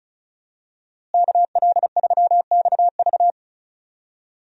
Morse code, heard as a keyed tone, about 36 wpm, 700 Hz. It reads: KL3XV
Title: USA Call Signs - 5 Characters - Repeated 3x - 35wpm